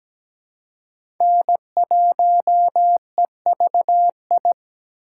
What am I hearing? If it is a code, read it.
Morse code, heard as a keyed tone, about 17 wpm, 700 Hz. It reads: N1EVI
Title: USA Call Signs - 5 Characters - 4x Character Spacing - 17wpm